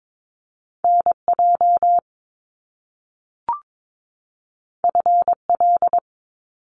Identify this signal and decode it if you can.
Morse code, heard as a keyed tone, about 22 wpm, 700 Hz. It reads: NJ FL
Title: US States 22wpm